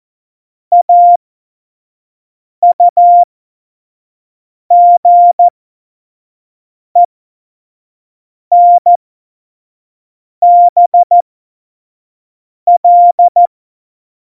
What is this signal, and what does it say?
Morse code, heard as a keyed tone, about 14 wpm, spaced as for 5 wpm, 700 Hz. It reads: AUGENBL
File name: Morsekurs / Witze / Witz 19. September 17, Tempo 25